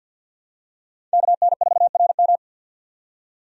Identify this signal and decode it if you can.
Morse code, heard as a keyed tone, about 51 wpm, 700 Hz. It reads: KN4RM
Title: USA Call Signs - 5 Characters - 50wpm